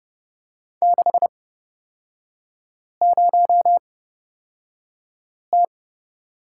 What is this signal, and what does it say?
Morse code, heard as a keyed tone, about 29 wpm, spaced as for 5 wpm, 690 Hz. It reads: 60T
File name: course / 30:10wpm / 236 Lesson 38e - Letter q - Three Characters 30wpm